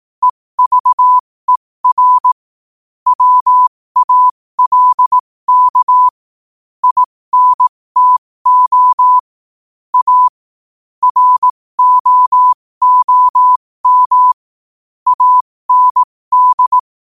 Morse code, heard as a keyed tone, about 18 wpm, spaced as for 15 wpm, 1000 Hz. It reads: EVER WALK INTO A ROOM AND
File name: qotdfiles / 15wpm